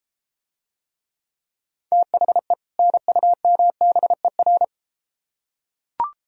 Morse code, heard as a keyed tone, about 33 wpm, 700 Hz. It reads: THE NUMBER